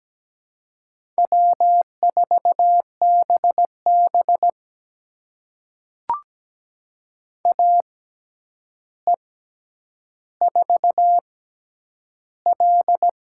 Morse code, heard as a keyed tone, about 17 wpm, spaced as for 9 wpm, 700 Hz. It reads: W4BB AE4L
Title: USA Call Signs - 4 Characters - 6x Character Spacing - 17wpm